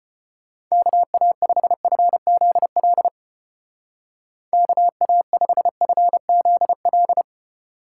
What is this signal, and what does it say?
Morse code, heard as a keyed tone, about 31 wpm, 700 Hz. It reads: KA5FZL KA5FZL